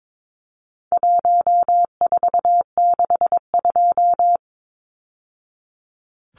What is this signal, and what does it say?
Morse code, heard as a keyed tone, about 22 wpm, 700 Hz. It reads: 1462
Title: Numbers - 4-Digit 22wpm